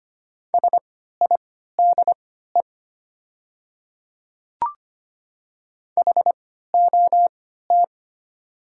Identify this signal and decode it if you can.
Morse code, heard as a keyed tone, about 25 wpm, spaced as for 14 wpm, 700 Hz. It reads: SIDE HOT
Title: 3x Character Spacing - Top 100 Words - 25wpm